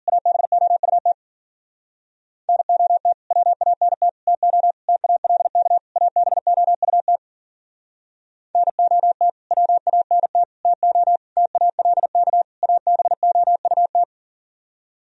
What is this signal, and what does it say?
Morse code, heard as a keyed tone, about 52 wpm, 700 Hz. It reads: ABOUT NOT WANT TO TALK ABOUT NOT WANT TO TALK ABOUT